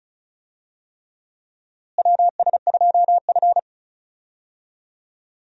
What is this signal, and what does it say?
Morse code, heard as a keyed tone, about 34 wpm, 700 Hz. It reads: WS2F
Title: USA Call Signs - 4 Characters - Repeated 3x - 35wpm